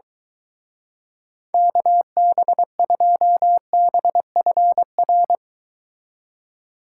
Morse code, heard as a keyed tone, about 23 wpm, 700 Hz. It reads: KB2BFR